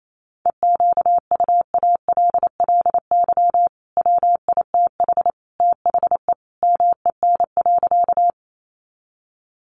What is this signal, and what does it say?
Morse code, heard as a keyed tone, about 28 wpm, 700 Hz. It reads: EQUALLY WITH THE MEN.
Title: Sentences from Top 1400 Words 28wpm